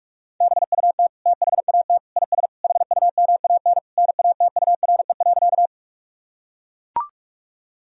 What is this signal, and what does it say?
Morse code, heard as a keyed tone, about 41 wpm, 700 Hz. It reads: BUT THAT IS HUMAN NATURE.
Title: Sentences from Top 1000 Words - Review Part 3 - 45wpm